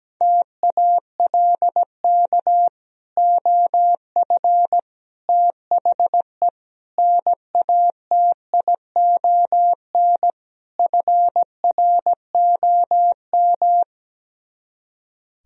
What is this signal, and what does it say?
Morse code, heard as a keyed tone, about 17 wpm, 700 Hz. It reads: TALK OF THE NATION FROM
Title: Speed-Racing - Sets of 5 Words 17wpm